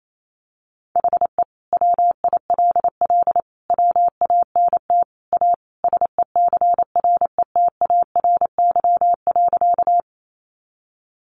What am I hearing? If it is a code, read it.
Morse code, heard as a keyed tone, about 28 wpm, 700 Hz. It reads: HE WILL WANT A SECRETARY.
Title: Sentences from Top 1300 Words 28wpm